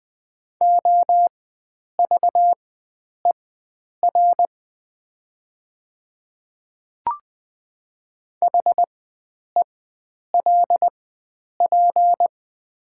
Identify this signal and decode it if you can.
Morse code, heard as a keyed tone, about 20 wpm, spaced as for 9 wpm, 700 Hz. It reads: OVER HELP
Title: 4x Character Spacing - Top 300 Words - 20wpm